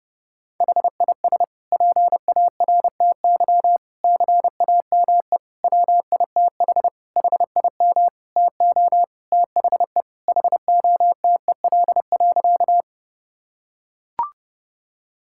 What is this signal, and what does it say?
Morse code, heard as a keyed tone, about 30 wpm, 700 Hz. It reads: HIS PARTY CAME WITH HIM TO THE HOTEL.
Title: Sentences from Top 1600 Words 30wpm